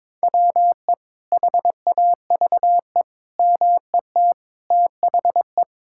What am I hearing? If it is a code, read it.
Morse code, heard as a keyed tone, about 22 wpm, 700 Hz. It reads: WE HAVE MET THE
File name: Sentences with 9-Word Phrases - 22wpm